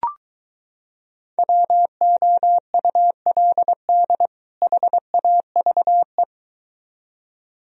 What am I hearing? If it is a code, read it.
Morse code, heard as a keyed tone, about 23 wpm, 700 Hz. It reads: WOULD HAVE